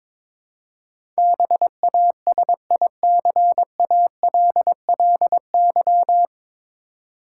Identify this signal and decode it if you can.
Morse code, heard as a keyed tone, about 22 wpm, 700 Hz. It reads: BASICALLY
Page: Morse Code Ninja practice 22wpm